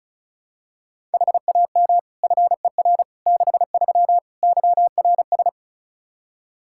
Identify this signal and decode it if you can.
Morse code, heard as a keyed tone, about 36 wpm, 700 Hz. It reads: HAM FER 63 YRS
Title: Practice QSO - Exchange 3 - More Info - 35wpm